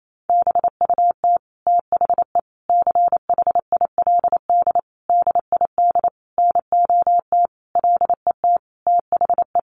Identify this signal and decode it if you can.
Morse code, heard as a keyed tone, about 28 wpm, 700 Hz. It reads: BUT THE CHILD DID NOT LET THE
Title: Sentences from Top 1400 Words 28wpm